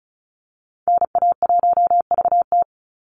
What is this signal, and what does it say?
Morse code, heard as a keyed tone, about 34 wpm, 700 Hz. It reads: NA1VT
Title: Call Signs Repeated 2x - 35wpm